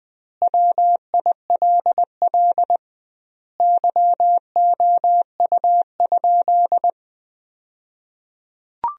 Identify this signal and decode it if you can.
Morse code, heard as a keyed tone, about 20 wpm, 700 Hz. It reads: WILL YOU?
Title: 2x Word-Spacing - Sentences from Top 300 Words - 20wpm